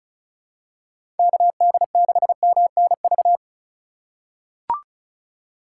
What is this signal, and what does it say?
Morse code, heard as a keyed tone, about 34 wpm, 700 Hz. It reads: KD6MNV
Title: USA Call Signs - 6 Characters - 3x Character Spacing - 35wpm